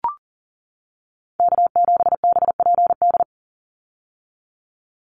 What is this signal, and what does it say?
Morse code, heard as a keyed tone, about 42 wpm, 700 Hz. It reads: K7BPD
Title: USA Call Signs 40wpm